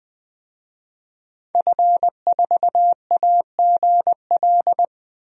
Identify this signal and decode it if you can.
Morse code, heard as a keyed tone, about 21 wpm, 700 Hz. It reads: F4AGL
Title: DXCC-100 Call Signs - 5 Characters - 20wpm